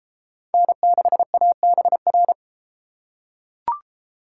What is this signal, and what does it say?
Morse code, heard as a keyed tone, about 32 wpm, 700 Hz. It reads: N6ABR